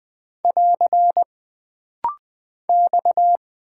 Morse code, heard as a keyed tone, about 20 wpm, 700 Hz. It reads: <AR> X
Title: Single Character (Letter:Number:Punctuation:Prosign) - Rapid-Fire - 20wpm